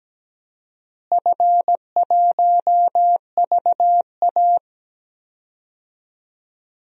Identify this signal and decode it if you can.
Morse code, heard as a keyed tone, about 17 wpm, 700 Hz. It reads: F1VA
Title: DXCC-100 Call Signs - 4 Characters - Repeated 3x - 17wpm